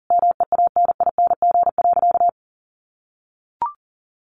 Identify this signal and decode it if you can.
Morse code, heard as a keyed tone, about 40 wpm, 700 Hz. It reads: MEANING.
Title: Sentences from Top 1100 Words 40wpm